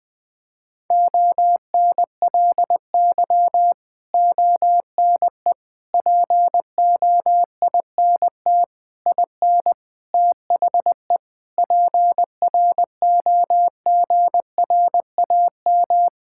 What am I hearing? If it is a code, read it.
Morse code, heard as a keyed tone, about 20 wpm, 700 Hz. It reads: ONLY ONE POINT IN THE PROGRAM
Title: Sentences from Top 1000 Words - Review Part 5 - 20wpm